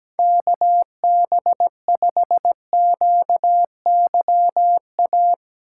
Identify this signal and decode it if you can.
Morse code, heard as a keyed tone, about 17 wpm, 700 Hz. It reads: KB5QYA
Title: USA Call Signs - 6 Characters - 2x Character Spacing - 17wpm